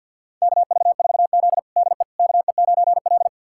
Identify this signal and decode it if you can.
Morse code, heard as a keyed tone, about 53 wpm, 700 Hz. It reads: KV4Z DE KE9L
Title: USA Callsign DE Callsign - 4 Characters - 50wpm